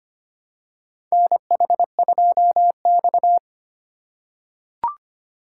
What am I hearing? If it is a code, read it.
Morse code, heard as a keyed tone, about 25 wpm, 700 Hz. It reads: NH2X
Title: USA Call Signs - 4 Characters - 2x Character Spacing - 25wpm